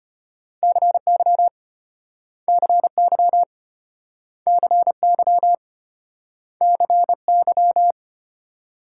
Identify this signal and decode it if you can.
Morse code, heard as a keyed tone, about 34 wpm, 700 Hz. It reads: CY CY CY CY